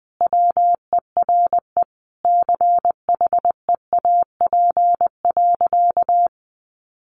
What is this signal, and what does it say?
Morse code, heard as a keyed tone, about 20 wpm, 700 Hz. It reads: WERE CHEAP.
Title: Sentences from Top 1200 Words 20wpm